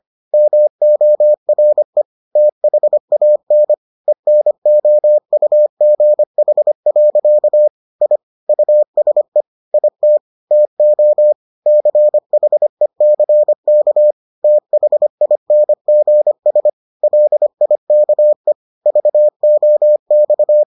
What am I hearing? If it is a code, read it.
Morse code, heard as a keyed tone, about 25 wpm, 600 Hz. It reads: MORE THAN ENOUGH. I USE IT TO CHECK THINGS LIKE VOX